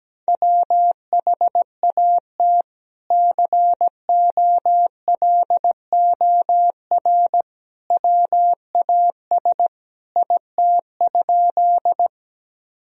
Morse code, heard as a keyed tone, about 17 wpm, 700 Hz. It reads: WHAT COLOR WAS IT?
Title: Sentences from Top 1000 Words - Review Part 2 - 17wpm